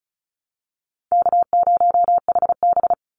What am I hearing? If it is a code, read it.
Morse code, heard as a keyed tone, about 34 wpm, 700 Hz. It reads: K0HB
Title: Call Signs Repeated 2x - 35wpm